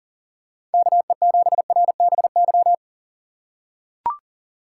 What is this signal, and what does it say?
Morse code, heard as a keyed tone, about 40 wpm, 700 Hz. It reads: KE7RBY
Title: USA Call Signs - 6 Characters - 2x Character Spacing - 40wpm